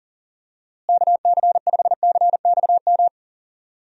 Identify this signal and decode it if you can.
Morse code, heard as a keyed tone, about 40 wpm, 700 Hz. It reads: KC5CXM
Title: USA Call Signs - 6 Characters - Repeated 3x - 40wpm